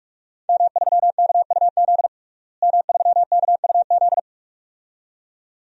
Morse code, heard as a keyed tone, about 42 wpm, 700 Hz. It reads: M3KUZ M3KUZ
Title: DXCC-100 Call Signs - 5 Characters - Repeated 2x - 45wpm